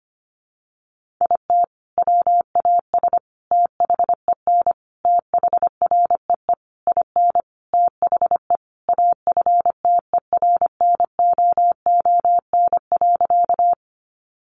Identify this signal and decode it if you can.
Morse code, heard as a keyed tone, about 25 wpm, 700 Hz. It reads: IT WAS THEN THREE IN THE AFTERNOON.
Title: Sentences from Top 1500 Words 25wpm